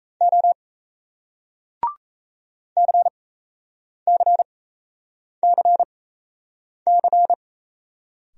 Morse code, heard as a keyed tone, about 38 wpm, spaced as for 8 wpm, 700 Hz. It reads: O CCCC